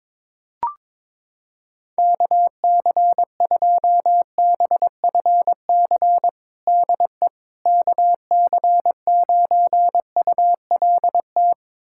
Morse code, heard as a keyed tone, about 21 wpm, 700 Hz. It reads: KC2BFC DE KC9ULT
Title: USA Callsign DE Callsign - 6 Characters - 22wpm